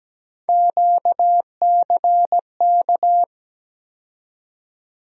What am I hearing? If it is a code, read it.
Morse code, heard as a keyed tone, about 17 wpm, 700 Hz. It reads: QCK
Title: USA Call Sign Postfixes - 3 Letters 17wpm